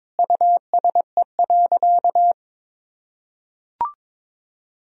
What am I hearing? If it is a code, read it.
Morse code, heard as a keyed tone, about 22 wpm, 700 Hz. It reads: USE.
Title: Sentences from Top 1000 Words - Review Part 4 - 22wpm